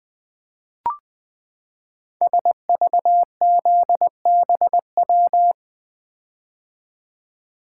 Morse code, heard as a keyed tone, about 20 wpm, 700 Hz. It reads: SVZBW
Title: Code Groups - 5 Letters - 20wpm